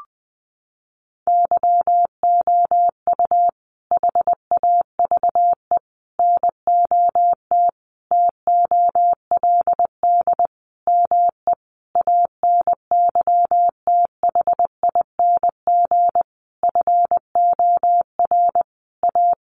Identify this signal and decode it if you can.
Morse code, heard as a keyed tone, about 20 wpm, 700 Hz. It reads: YOU HAVE NOT TOLD ME ANYTHING FOR A
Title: Sentences from Top 1000 Words 20wpm